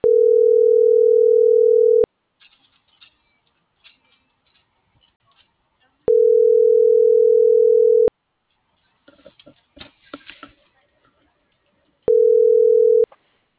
Ringback tone (North America), 440 + 480 Hz.